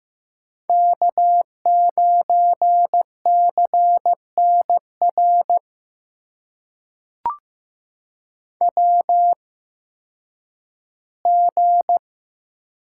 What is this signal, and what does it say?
Morse code, heard as a keyed tone, about 15 wpm, spaced as for 10 wpm, 700 Hz. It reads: K9CNR WG
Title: USA Call Signs - 8x Character Spacing - 15wpm